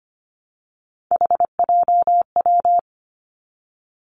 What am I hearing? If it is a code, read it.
Morse code, heard as a keyed tone, about 25 wpm, 700 Hz. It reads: HJW
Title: Three Challenging Letter-Number Combinations 25wpm